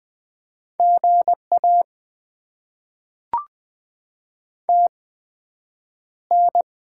Morse code, heard as a keyed tone, about 20 wpm, 700 Hz. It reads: G A T N
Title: Top 20 QSO Elements - 8x Character Spacing - 20wpm